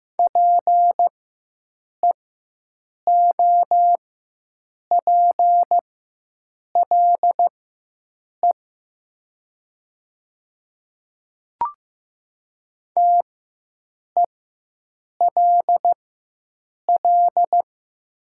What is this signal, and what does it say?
Morse code, heard as a keyed tone, about 15 wpm, spaced as for 7 wpm, 700 Hz. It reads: PEOPLE TELL
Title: 4x Character Spacing - Top 300 Words - 15wpm